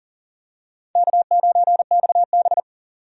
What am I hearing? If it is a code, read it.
Morse code, heard as a keyed tone, about 40 wpm, 700 Hz. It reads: K9XB
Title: USA Call Signs - 4 Characters - 40wpm